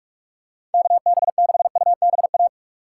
Morse code, heard as a keyed tone, about 44 wpm, 700 Hz. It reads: KB6UBA
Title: USA Call Signs - 6x Character Spacing - 45wpm